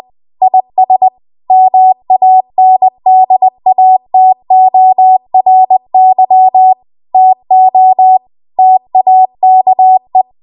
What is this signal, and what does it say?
Morse code, heard as a keyed tone, about 20 wpm, 750 Hz. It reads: IS MANDATORY TO TAKE
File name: Morse / 20WPM